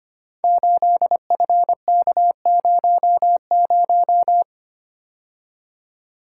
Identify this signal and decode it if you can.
Morse code, heard as a keyed tone, about 25 wpm, 700 Hz. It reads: 8FK00
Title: Code Groups - 5 Letter:Numbers - 25wpm